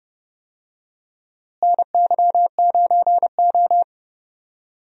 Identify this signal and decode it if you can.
Morse code, heard as a keyed tone, about 30 wpm, 700 Hz. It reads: NY9O
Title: USA Call Signs - 4 Characters - Repeated 3x - 30wpm